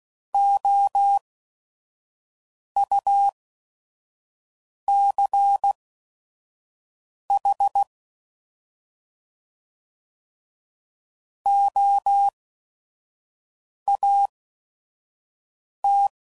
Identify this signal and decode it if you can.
Morse code, heard as a keyed tone, about 16 wpm, spaced as for 5 wpm, 800 Hz. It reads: OUCH OAT
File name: Prefixes / By First Letter / Os